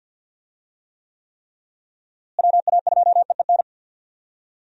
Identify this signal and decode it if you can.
Morse code, heard as a keyed tone, about 46 wpm, 700 Hz. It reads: WA2EEN
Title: USA Call Signs - 6 Characters - Repeated 3x - 50wpm